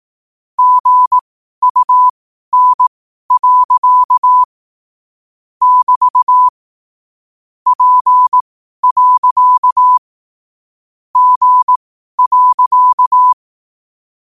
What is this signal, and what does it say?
Morse code, heard as a keyed tone, about 18 wpm, spaced as for 12 wpm, 1000 Hz. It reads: GUN. <BT> P. G.
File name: qotdfiles / 12wpm